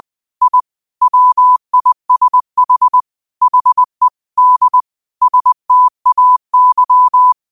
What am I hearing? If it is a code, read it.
Morse code, heard as a keyed tone, about 20 wpm, 1000 Hz. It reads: I WISH HE D STAY
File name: qotdfiles / 20wpm